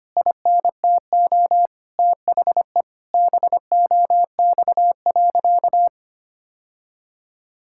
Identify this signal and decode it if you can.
Morse code, heard as a keyed tone, about 25 wpm, 700 Hz. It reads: INTO THE BOX.
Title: Sentences from Top 1000 Words - Review Part 6 - 25wpm